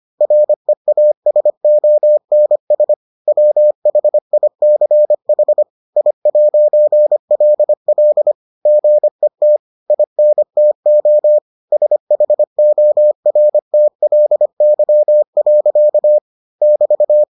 Morse code, heard as a keyed tone, about 25 wpm, 600 Hz. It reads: REASONS WHICH I'LL GET INTO SHORTLY. <BT>